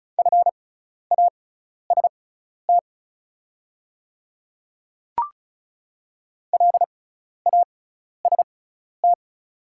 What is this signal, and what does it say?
Morse code, heard as a keyed tone, about 36 wpm, spaced as for 12 wpm, 690 Hz. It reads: FAST LAST
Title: 6x Character Spacing - Top 500 Words - 35wpm